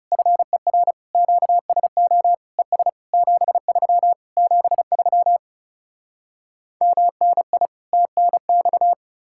Morse code, heard as a keyed tone, about 40 wpm, 700 Hz. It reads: FER QSO ES 73 73 MNI TNX